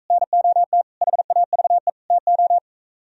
Morse code, heard as a keyed tone, about 44 wpm, 700 Hz. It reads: NOT HAVE TO